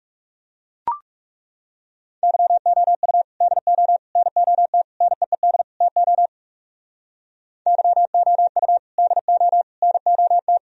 Morse code, heard as a keyed tone, about 37 wpm, 700 Hz. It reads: YOU DO NOT NEED TO YOU DO NOT
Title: Speed-Racing - Sets of 5 Words 30wpm